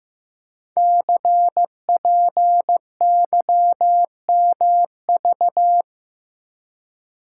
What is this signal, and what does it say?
Morse code, heard as a keyed tone, about 15 wpm, 700 Hz. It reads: CPYMV
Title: Code Groups - 5 Letter:Numbers - 15wpm